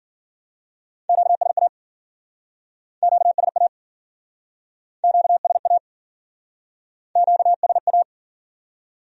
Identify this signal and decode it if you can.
Morse code, heard as a keyed tone, about 52 wpm, 700 Hz. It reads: QSA QSA QSA QSA